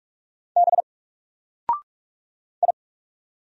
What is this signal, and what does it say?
Morse code, heard as a keyed tone, about 44 wpm, 700 Hz. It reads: B I
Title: Single Character (Letter:Number:Punctuation:Prosign) 45wpm